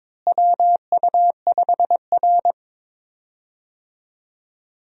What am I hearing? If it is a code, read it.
Morse code, heard as a keyed tone, about 21 wpm, 700 Hz. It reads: WU5R
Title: USA Call Signs - 4 Characters - Repeated 3x - 22wpm